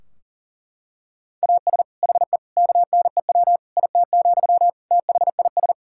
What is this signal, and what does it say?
Morse code, heard as a keyed tone, about 40 wpm, 700 Hz. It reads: AS HE KNEW IT, THIS
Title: Sentences from Top 1600 Words 40wpm